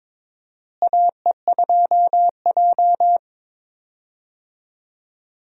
Morse code, heard as a keyed tone, about 22 wpm, 700 Hz. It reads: AE2J
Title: USA Call Signs - 4 Characters - Repeated 3x - 22wpm